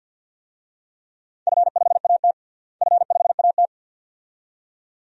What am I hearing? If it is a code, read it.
Morse code, heard as a keyed tone, about 51 wpm, 700 Hz. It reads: F5AT F5AT
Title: DXCC-100 Call Signs - 4 Characters - Repeated 2x - 50wpm